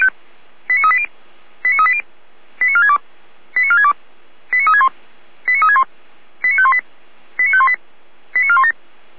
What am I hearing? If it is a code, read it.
Five-tone selective call (ZVEI), heard as quick runs of five tones, tones of about 70 ms. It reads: 89380 89380 89462 89462 89361 89361 89518 89518 89417